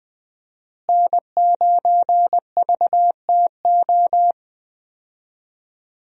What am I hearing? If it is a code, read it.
Morse code, heard as a keyed tone, about 20 wpm, 700 Hz. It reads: N9VTO
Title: USA Call Signs - 5 Characters - Repeated 3x - 20wpm